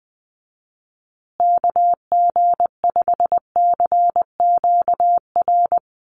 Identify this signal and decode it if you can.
Morse code, heard as a keyed tone, about 20 wpm, 700 Hz. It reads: KG5CQR
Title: USA Call Signs Repeated 3x 20wpm